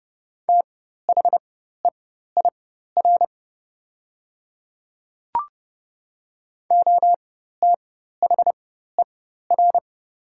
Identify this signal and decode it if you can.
Morse code, heard as a keyed tone, about 30 wpm, spaced as for 14 wpm, 700 Hz. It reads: THEIR OTHER
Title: 4x Character Spacing - Top 500 Words - 30wpm